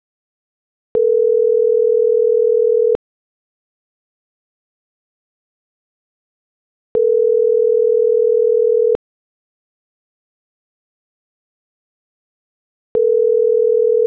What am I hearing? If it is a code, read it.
Ringback tone (North America), 440 + 480 Hz.